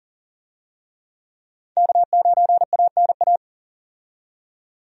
Morse code, heard as a keyed tone, about 40 wpm, 700 Hz. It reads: K9ANA
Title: USA Call Signs - 5 Characters - 40wpm